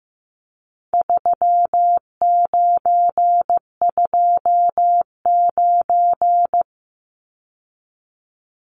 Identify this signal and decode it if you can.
Morse code, heard as a keyed tone, about 15 wpm, 700 Hz. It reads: 3929
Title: Numbers - 4-Digit 15wpm